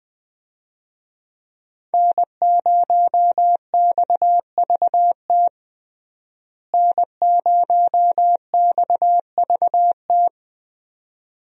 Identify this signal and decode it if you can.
Morse code, heard as a keyed tone, about 20 wpm, 700 Hz. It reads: N0XVT N0XVT
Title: USA Call Signs - 5 Characters - Repeated 2x - 20wpm